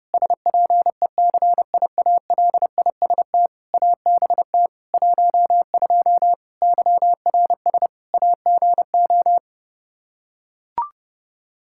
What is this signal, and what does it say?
Morse code, heard as a keyed tone, about 30 wpm, 700 Hz. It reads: SPECIALIST ABT 12 YRS AGO
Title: Practice QSO - Exchange 3 - More Info - 30wpm